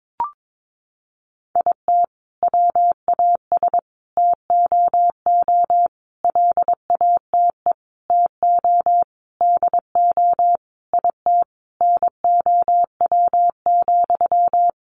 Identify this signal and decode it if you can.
Morse code, heard as a keyed tone, about 22 wpm, 700 Hz. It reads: IT WAS TOO LATE TO DO IT NOW,
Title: Sentences from Top 600 Words 22wpm